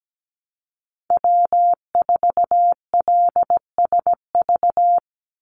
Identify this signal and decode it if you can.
Morse code, heard as a keyed tone, about 17 wpm, 700 Hz. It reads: W4LSV
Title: Call Signs Repeated 3x - 17wpm